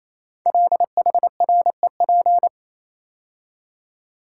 Morse code, heard as a keyed tone, about 29 wpm, 700 Hz. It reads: LHREP
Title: Code Groups - 5 Letters - 28wpm